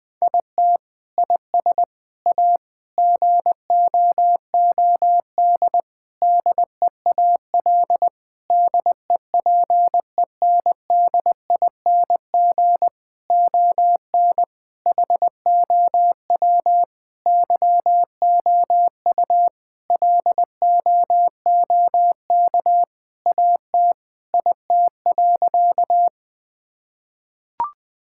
Morse code, heard as a keyed tone, about 21 wpm, 700 Hz. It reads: IT IS A GOOD DEAL DEPENDING ON HOW YOU LOOK AT IT.
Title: Sentences with 7-Word Phrases - Spoken Beginning - 20wpm